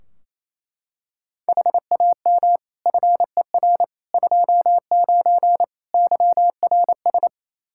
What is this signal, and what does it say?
Morse code, heard as a keyed tone, about 27 wpm, 700 Hz. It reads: HAM FER 29 YRS
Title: Practice QSO - Exchange 3 - More Info - 28wpm